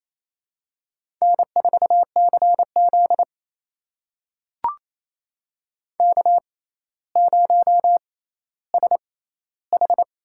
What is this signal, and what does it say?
Morse code, heard as a keyed tone, about 28 wpm, spaced as for 14 wpm, 700 Hz. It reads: N4CZ K0SH
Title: USA Call Signs - 4 Characters - 6x Character Spacing - 28wpm